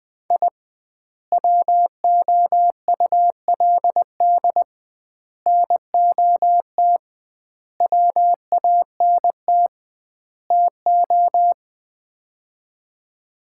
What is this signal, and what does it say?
Morse code, heard as a keyed tone, about 21 wpm, 700 Hz. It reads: I WOULD NOT WANT TO
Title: 2x Word-Spacing - Sets of 5 Words - 20wpm